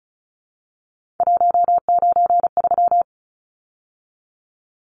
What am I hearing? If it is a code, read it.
Morse code, heard as a keyed tone, about 34 wpm, 700 Hz. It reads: 193
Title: Numbers - 3-Digit 35wpm